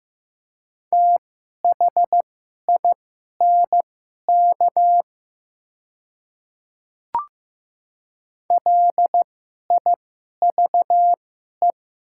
Morse code, heard as a keyed tone, about 15 wpm, spaced as for 11 wpm, 700 Hz. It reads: THINK LIVE
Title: 2x Character Spacing - Top 500 Words - 15wpm